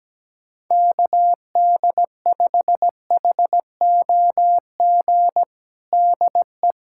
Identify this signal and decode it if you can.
Morse code, heard as a keyed tone, about 17 wpm, 700 Hz. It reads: KD5HOG DE
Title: USA Callsign DE Callsign - 6 Characters - 17wpm